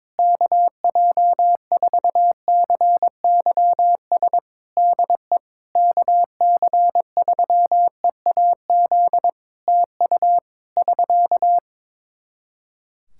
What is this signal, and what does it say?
Morse code, heard as a keyed tone, about 22 wpm, 700 Hz. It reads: KJ4CYS DE KC3EAZ TU <SK>
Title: Practice QSO - Exchange 4 - Ending - 22wpm